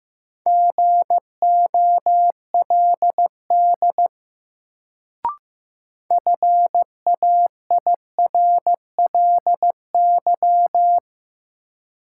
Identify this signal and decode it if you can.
Morse code, heard as a keyed tone, about 15 wpm, 700 Hz. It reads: GOLD FAIRLY
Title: Sentences from Top 1100 Words - New Word Drill Alt - 15wpm